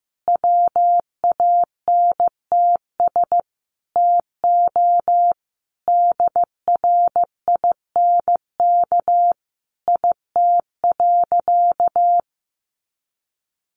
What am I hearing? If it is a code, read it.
Morse code, heard as a keyed tone, about 15 wpm, 700 Hz. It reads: WANTS TO DRINK IT.
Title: Additional Easy Phrases 15wpm